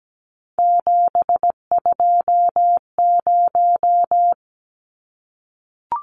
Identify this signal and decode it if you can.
Morse code, heard as a keyed tone, about 17 wpm, 700 Hz. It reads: 720
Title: Numbers - 3-Digit 17wpm